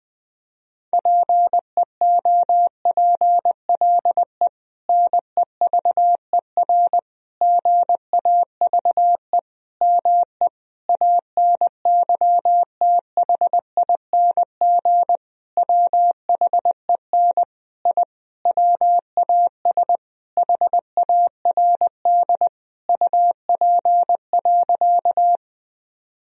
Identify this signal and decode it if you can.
Morse code, heard as a keyed tone, about 19 wpm, 700 Hz. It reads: PEOPLE NEVER GAVE ME ANYTHING WHEN I WAS HARD UP.
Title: Sentences from Top 1000 Words - Review Part 1 - 20wpm